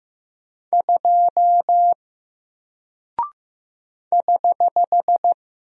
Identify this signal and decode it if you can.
Morse code, heard as a keyed tone, about 15 wpm, 700 Hz. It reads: 2 <HH>
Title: Single Character (Letter:Number:Punctuation:Prosign) Alt 15wpm